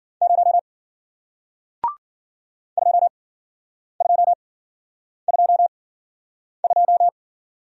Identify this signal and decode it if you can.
Morse code, heard as a keyed tone, about 53 wpm, spaced as for 9 wpm, 700 Hz. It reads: 0 2222